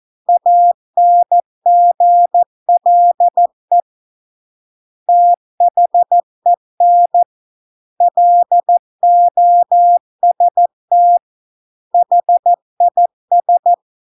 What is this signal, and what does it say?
Morse code, heard as a keyed tone, about 14 wpm, 700 Hz. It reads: ANGLE THEN LOST HIS